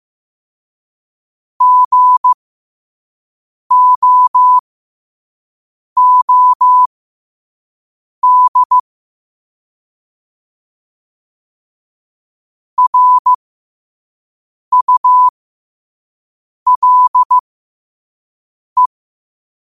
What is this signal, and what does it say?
Morse code, heard as a keyed tone, about 15 wpm, spaced as for 5 wpm, 1000 Hz. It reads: GOOD RULE
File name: qotdfiles / 5wpm